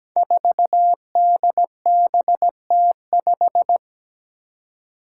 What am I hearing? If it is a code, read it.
Morse code, heard as a keyed tone, about 17 wpm, 700 Hz. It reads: 4DBT5
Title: Five Challenging Letter-Number Combinations 17wpm